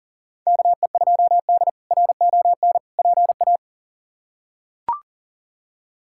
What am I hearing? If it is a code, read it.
Morse code, heard as a keyed tone, about 40 wpm, 700 Hz. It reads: KE2D RON PA
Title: K1USN SST QSO Practice - 40wpm